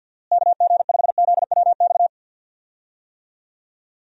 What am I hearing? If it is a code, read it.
Morse code, heard as a keyed tone, about 49 wpm, 700 Hz. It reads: KG5ZWX
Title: USA Call Signs - 6 Characters - Repeated 2x - 50wpm